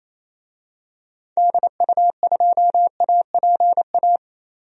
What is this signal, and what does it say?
Morse code, heard as a keyed tone, about 28 wpm, 700 Hz. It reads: DU2APA
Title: DXCC-100 Call Signs - 6 Characters - Repeated 3x - 28wpm